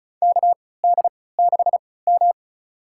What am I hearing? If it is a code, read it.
Morse code, heard as a keyed tone, about 34 wpm, spaced as for 20 wpm, 700 Hz. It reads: KD6M
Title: USA Call Signs - 4 Characters - 3x Character Spacing - 35wpm